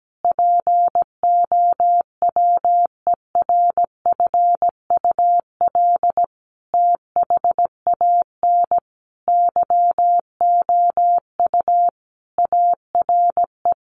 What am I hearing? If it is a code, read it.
Morse code, heard as a keyed tone, about 17 wpm, 700 Hz. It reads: POWERFUL THAN YOU ARE
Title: Sentences from Top 1300 Words 17wpm